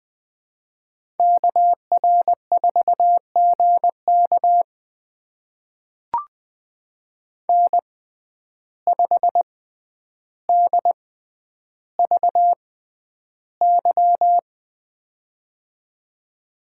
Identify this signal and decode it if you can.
Morse code, heard as a keyed tone, about 20 wpm, spaced as for 10 wpm, 700 Hz. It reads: KR4GK N5DVY
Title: USA Call Signs - 5 Characters - 6x Character Spacing - 20wpm